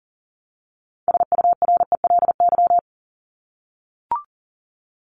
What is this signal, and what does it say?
Morse code, heard as a keyed tone, about 40 wpm, 700 Hz. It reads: SURELY
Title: LY Suffix - Top 100 Words with Base Words 40wpm